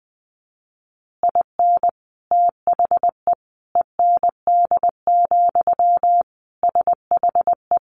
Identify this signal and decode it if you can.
Morse code, heard as a keyed tone, about 20 wpm, 700 Hz. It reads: IN THE END, SHE
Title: Additional Easy Phrases 20wpm